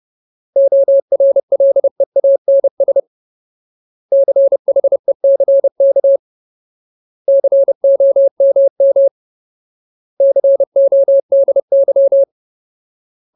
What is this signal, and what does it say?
Morse code, heard as a keyed tone, about 30 wpm, 550 Hz. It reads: ORLEANS CHECK COMM CODY
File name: Ultimate Word List / 30WPM Ultimate